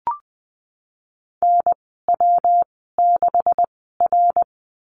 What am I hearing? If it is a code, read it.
Morse code, heard as a keyed tone, about 20 wpm, spaced as for 15 wpm, 700 Hz. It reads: NW6R
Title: 147 Lesson 24g - Letter y - Call Signs 20wpm